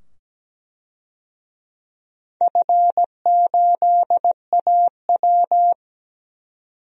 Morse code, heard as a keyed tone, about 17 wpm, 700 Hz. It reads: F8AW
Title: DXCC-100 Call Signs - 4 Characters - 17wpm